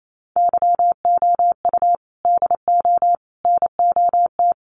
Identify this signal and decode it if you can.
Morse code, heard as a keyed tone, about 28 wpm, 700 Hz. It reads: YOU DO NOT
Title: Sets of 5 Words 28wpm